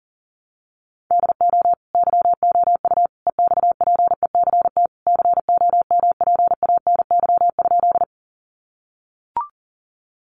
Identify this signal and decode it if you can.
Morse code, heard as a keyed tone, about 40 wpm, 700 Hz. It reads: DO YOU EXPECT COMPANY?
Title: Sentences from Top 1100 Words 40wpm